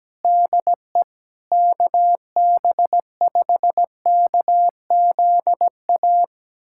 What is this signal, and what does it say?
Morse code, heard as a keyed tone, about 17 wpm, 700 Hz. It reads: DE KB5KZA
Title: USA Callsign DE Callsign - 6 Characters - 17wpm